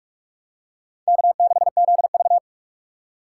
Morse code, heard as a keyed tone, about 46 wpm, 700 Hz. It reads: K6ZV
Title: USA Call Signs - 3x Character Spacing - 45wpm